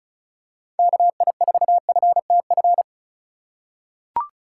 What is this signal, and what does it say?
Morse code, heard as a keyed tone, about 36 wpm, 700 Hz. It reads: KI4FTF
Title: USA Call Signs - 6 Characters - 2x Character Spacing - 35wpm